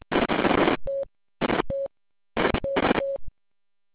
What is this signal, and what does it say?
There is no code.